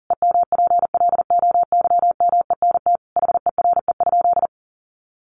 Morse code, heard as a keyed tone, about 40 wpm, 700 Hz. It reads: EMPLOYMENT HERE?
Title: Sentences from Top 1500 Words 40wpm